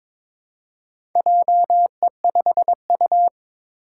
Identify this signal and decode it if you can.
Morse code, heard as a keyed tone, about 22 wpm, 700 Hz. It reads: JE5U
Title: Four Challenging Letter-Number Combinations 22wpm